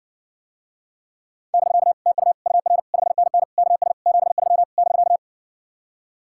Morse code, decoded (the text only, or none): <BK> TU UR 5NN KS 73 <BK>